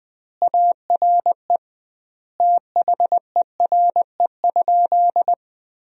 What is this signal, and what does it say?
Morse code, heard as a keyed tone, about 20 wpm, 700 Hz. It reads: ARE THERE?
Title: 2x Word-Spacing - Sentences from Top 100 Words - 20wpm